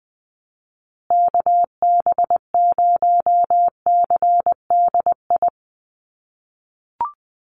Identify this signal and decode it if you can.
Morse code, heard as a keyed tone, about 20 wpm, 700 Hz. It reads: KB0CDI
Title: USA Call Signs Repeated 2x 20wpm